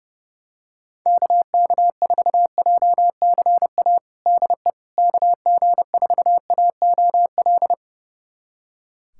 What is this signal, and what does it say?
Morse code, heard as a keyed tone, about 32 wpm, 700 Hz. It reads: KK4JCA DE KG4AOL
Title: USA Callsign DE Callsign - 6 Characters - 30wpm